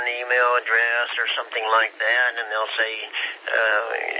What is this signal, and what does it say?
Emergency Alert System (SAME) tones, heard as harsh buzzing bursts, with speech. It is heard from 0.9 to 3.8 s.